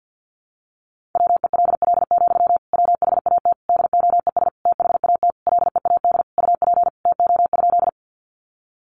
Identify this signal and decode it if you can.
Morse code, heard as a keyed tone, about 50 wpm, 700 Hz. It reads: WELL, WHAT DOES THAT LEAD UP TO?
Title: Sentences from Top 500 Words 50wpm